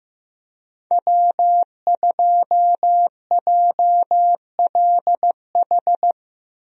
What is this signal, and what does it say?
Morse code, heard as a keyed tone, about 15 wpm, 700 Hz. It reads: W2JLH
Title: USA Call Signs - 5 Characters - 15wpm